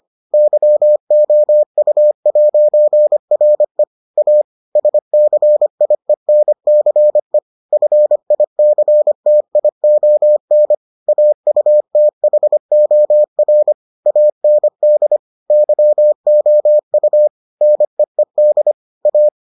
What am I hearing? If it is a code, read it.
Morse code, heard as a keyed tone, about 25 wpm, 600 Hz. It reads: YOU'RE A SCIENCE FICTION AUTHOR AND YOU NEED A